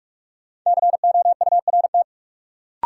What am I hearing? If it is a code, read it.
Morse code, heard as a keyed tone, about 46 wpm, 700 Hz. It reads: COURT
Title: Sentences from Top 1600 Words - New Word Drill Alt - 45wpm